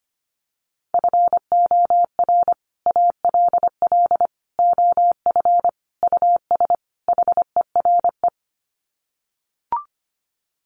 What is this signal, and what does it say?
Morse code, heard as a keyed tone, about 25 wpm, 700 Hz. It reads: FOR ALL OF US HERE
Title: Sets of 5 Words 25wpm